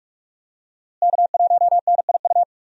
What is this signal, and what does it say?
Morse code, heard as a keyed tone, about 46 wpm, 700 Hz. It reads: K1NIU
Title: USA Call Signs - 5 Characters - Repeated 2x - 45wpm